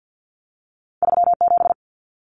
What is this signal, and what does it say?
Morse code, heard as a keyed tone, about 50 wpm, 700 Hz. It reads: 37
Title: Numbers - 2-Digit 50wpm